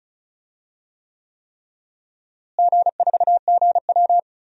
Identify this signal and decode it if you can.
Morse code, heard as a keyed tone, about 36 wpm, 700 Hz. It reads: G4GW
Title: DXCC-100 Call Signs - 4 Characters - Repeated 3x - 35wpm